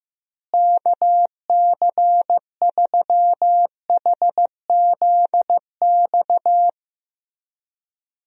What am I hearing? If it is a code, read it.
Morse code, heard as a keyed tone, about 16 wpm, 700 Hz. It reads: KC3HZX
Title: USA Call Signs - 6 Characters - Repeated 3x - 15wpm